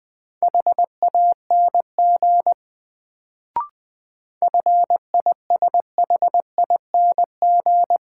Morse code, heard as a keyed tone, about 20 wpm, 700 Hz. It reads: HANG FISHING
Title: Sentences from Top 1600 Words - New Word Drill Alt - 20wpm